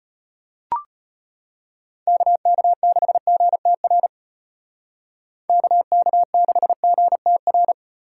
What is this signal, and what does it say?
Morse code, heard as a keyed tone, about 34 wpm, 700 Hz. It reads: KK6GTR KK6GTR